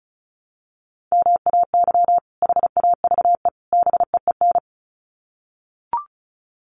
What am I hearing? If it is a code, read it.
Morse code, heard as a keyed tone, about 34 wpm, 700 Hz. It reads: MAY HAVE BEEN
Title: Sets of 3 Words 35wpm